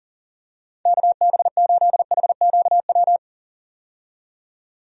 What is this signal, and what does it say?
Morse code, heard as a keyed tone, about 40 wpm, 700 Hz. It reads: KB8HQW
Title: USA Call Signs - 6 Characters - 40wpm